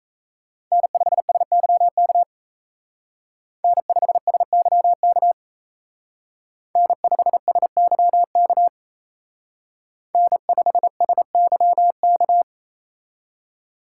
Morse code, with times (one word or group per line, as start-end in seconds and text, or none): 0.7-2.2 N5SYK
3.6-5.3 N5SYK
6.8-8.7 N5SYK
10.1-12.4 N5SYK